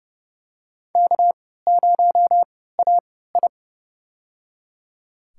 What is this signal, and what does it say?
Morse code, heard as a keyed tone, about 32 wpm, spaced as for 17 wpm, 700 Hz. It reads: K0AI
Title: USA Call Signs - 4 Characters - 3x Character Spacing - 30wpm